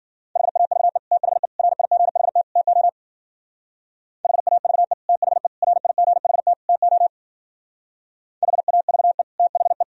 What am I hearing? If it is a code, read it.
Morse code, heard as a keyed tone, about 50 wpm, 700 Hz. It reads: HAVE THE RIGHT TO HAVE THE RIGHT TO HAVE THE
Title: Speed-Racing - Sets of 4 Words 40wpm